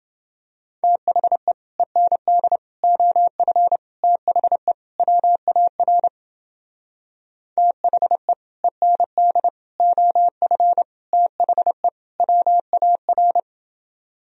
Morse code, heard as a keyed tone, about 30 wpm, 700 Hz. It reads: THE END OF THE WAR THE END OF THE WAR